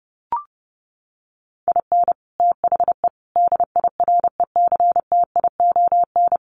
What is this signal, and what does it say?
Morse code, heard as a keyed tone, about 30 wpm, 700 Hz. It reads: IN THE DIRECTION
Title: Sets of 5 Words 30wpm